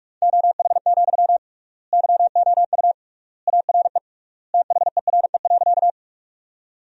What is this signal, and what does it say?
Morse code, heard as a keyed tone, about 46 wpm, 700 Hz. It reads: OH, YOU ARE THERE.
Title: 3x Word-Spacing - Sentences from Top 500 Words - 45wpm